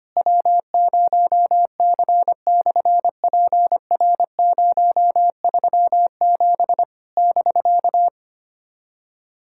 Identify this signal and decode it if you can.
Morse code, heard as a keyed tone, about 25 wpm, 700 Hz. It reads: W0C/PR037 <BK>
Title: SOTA QSO Practice 25wpm